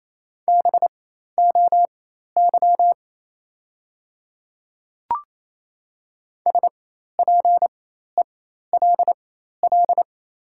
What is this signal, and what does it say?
Morse code, heard as a keyed tone, about 27 wpm, spaced as for 13 wpm, 700 Hz. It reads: BOY SPELL
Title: 4x Character Spacing - Top 200 Words - 28wpm